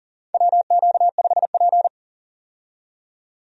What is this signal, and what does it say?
Morse code, heard as a keyed tone, about 37 wpm, 700 Hz. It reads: WQ5P
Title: USA Call Signs - 4 Characters - 40wpm